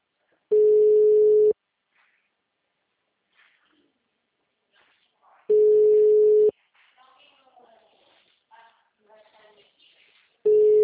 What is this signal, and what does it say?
Ringback tone (Europe, 425 Hz), with speech.